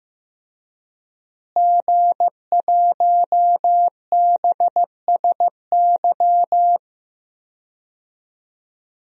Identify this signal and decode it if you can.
Morse code, heard as a keyed tone, about 15 wpm, 700 Hz. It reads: G1BSY
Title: DXCC-100 Call Signs - 5 Characters - Repeated 3x - 15wpm